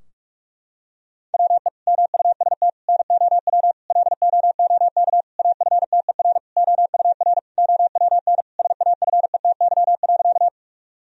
Morse code, heard as a keyed tone, about 46 wpm, 700 Hz. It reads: WE MUST NOW LOOK AFTER OUR OWN SAFETY.
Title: Sentences from Top 1000 Words - Review Part 4 - 45wpm